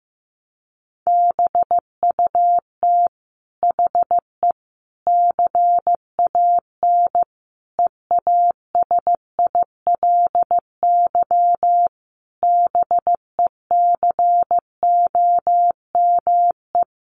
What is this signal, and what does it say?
Morse code, heard as a keyed tone, about 15 wpm, 700 Hz. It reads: BUT HE CAN EASILY BECOME
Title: Sentences from Top 900 Words 15wpm